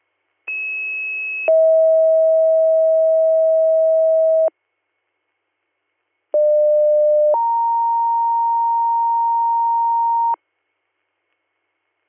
Two-tone page, heard as a one-second tone, then a three-second tone, about 2600 then 640 Hz, about 600 then 930 Hz. It is heard from 0.5 to 10.3 s.